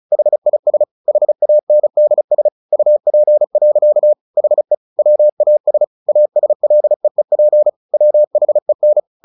Morse code, heard as a keyed tone, about 34 wpm, 600 Hz. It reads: HIS HANDS UP. HE WAS ASLEEP WHEN
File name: the old man and the sea 2 35wpm